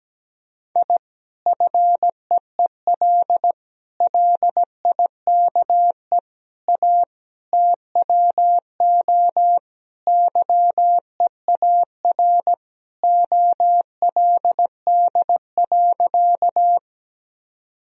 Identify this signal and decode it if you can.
Morse code, heard as a keyed tone, about 17 wpm, 700 Hz. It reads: I FEEL LIKE A TWO YEAR OLD.
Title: Sentences from Top 1000 Words - Review Part 1 - 17wpm